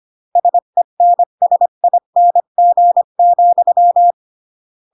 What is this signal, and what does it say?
Morse code, heard as a keyed tone, about 25 wpm, 700 Hz. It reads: SENSING,
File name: CW / jodorowsky / 82 commandments 25-20